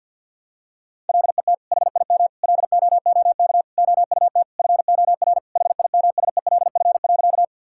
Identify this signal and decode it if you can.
Morse code, heard as a keyed tone, about 50 wpm, 700 Hz. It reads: LET HIM LOOK OUT FOR HIMSELF.
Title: Sentences from Top 1600 Words 50wpm